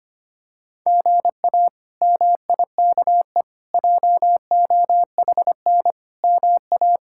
Morse code, heard as a keyed tone, about 25 wpm, 700 Hz. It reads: GA MIKE JOHN MA
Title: K1USN SST QSO Practice - 25wpm